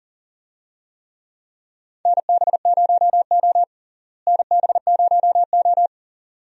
Morse code, heard as a keyed tone, about 40 wpm, 700 Hz. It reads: NB0O NB0O
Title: USA Call Signs - 4 Characters - Repeated 2x - 40wpm